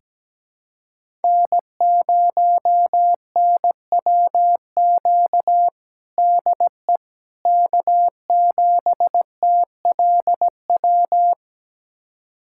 Morse code, heard as a keyed tone, about 17 wpm, 700 Hz. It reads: N0NWQ DE K7TLW
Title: USA Callsign DE Callsign - 5 Characters - 17wpm